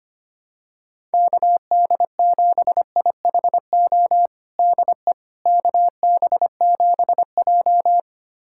Morse code, heard as a keyed tone, about 25 wpm, 700 Hz. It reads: KD7IHO DE KB7J
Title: USA Callsign DE Callsign 25wpm